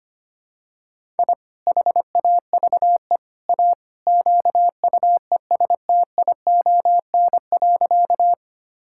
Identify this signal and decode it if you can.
Morse code, heard as a keyed tone, about 25 wpm, 700 Hz. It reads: I HAVE A QUESTION.
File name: practice / 25wpm / Sentences with 3-Word Phrases - 25wpm